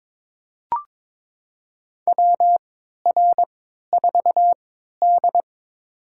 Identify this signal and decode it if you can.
Morse code, heard as a keyed tone, about 22 wpm, spaced as for 13 wpm, 700 Hz. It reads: WR4D
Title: USA Call Signs - 4 Characters - 3x Character Spacing - 22wpm